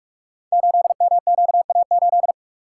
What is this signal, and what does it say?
Morse code, heard as a keyed tone, about 44 wpm, 700 Hz. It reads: 8MQA8
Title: Code Groups - 5 Letter:Numbers - 45wpm